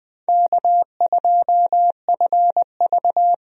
Morse code, heard as a keyed tone, about 20 wpm, 700 Hz. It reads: K2FV
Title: USA Call Signs - 4 Characters - Repeated 3x - 20wpm